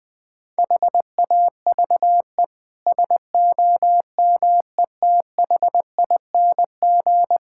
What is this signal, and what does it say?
Morse code, heard as a keyed tone, about 21 wpm, 700 Hz. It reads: HAVE SOMETHING
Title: Sets of 8 Words - 20wpm